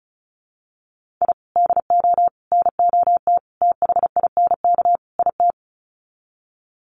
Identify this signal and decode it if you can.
Morse code, heard as a keyed tone, about 34 wpm, 700 Hz. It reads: I DO NOT THINK IT
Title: Sets of 5 Words 35wpm